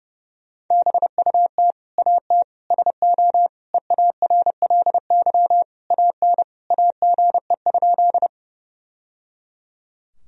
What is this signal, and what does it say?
Morse code, heard as a keyed tone, about 29 wpm, 700 Hz. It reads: BUT AT SO EARLY AN AGE?
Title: Sentences from Top 1000 Words - Review Part 3 - 30wpm